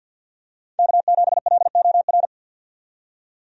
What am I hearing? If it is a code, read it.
Morse code, heard as a keyed tone, about 51 wpm, 700 Hz. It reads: K7LOR
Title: USA Call Signs - 5 Characters - 50wpm